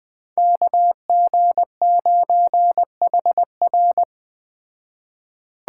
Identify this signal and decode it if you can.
Morse code, heard as a keyed tone, about 20 wpm, 700 Hz. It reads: KG9HR
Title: Speed Racing - USA 2x2 Call Signs 17wpm